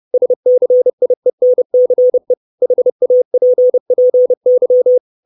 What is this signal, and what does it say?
Morse code, heard as a keyed tone, about 30 wpm, 500 Hz. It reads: SCIENCE HAPPY
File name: 632 Words 30WPM